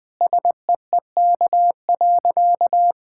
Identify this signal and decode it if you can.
Morse code, heard as a keyed tone, about 20 wpm, 700 Hz. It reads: SEEK.
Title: Sentences from Top 1600 Words 20wpm